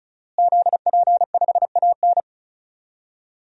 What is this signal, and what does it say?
Morse code, heard as a keyed tone, about 36 wpm, 700 Hz. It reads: ZP5AN
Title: DXCC-100 Call Signs - 5 Characters - Repeated 3x - 35wpm